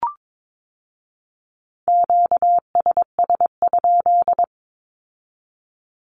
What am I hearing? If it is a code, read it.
Morse code, heard as a keyed tone, about 22 wpm, 700 Hz. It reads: QSS?
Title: Q-Codes with Spoken Meaning - 22wpm